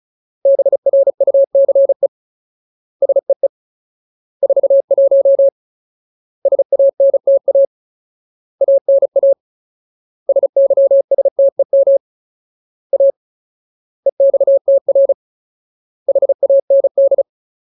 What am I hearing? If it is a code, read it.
Morse code, heard as a keyed tone, about 34 wpm, 550 Hz. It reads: BRUCE SEE 41 SANTA ANA SYSTEM A EXTR HAND
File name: Ultimate Word List / 35WPM Ultimate